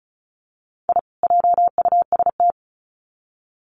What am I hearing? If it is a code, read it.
Morse code, heard as a keyed tone, about 34 wpm, 700 Hz. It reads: I JUST
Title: Sets of 2 Words 35wpm